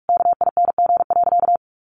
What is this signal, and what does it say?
Morse code, heard as a keyed tone, about 44 wpm, 700 Hz. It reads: KING.
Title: Sentences from Top 600 Words 45wpm